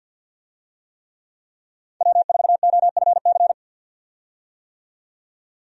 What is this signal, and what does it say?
Morse code, heard as a keyed tone, about 51 wpm, 700 Hz. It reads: W4OFC
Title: USA Call Signs - 5 Characters - 50wpm